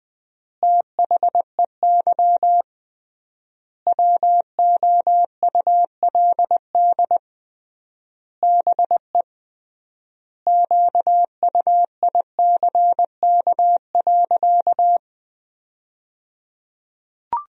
Morse code, heard as a keyed tone, about 21 wpm, 700 Hz. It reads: THEY WOULD BE QUICK.
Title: 3x Word-Spacing - Sentences from Top 500 Words - 20wpm